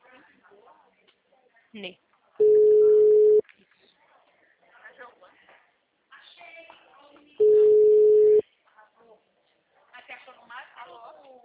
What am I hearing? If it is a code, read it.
Ringback tone (Europe, 425 Hz), with speech.
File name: out- -2007- - - .56644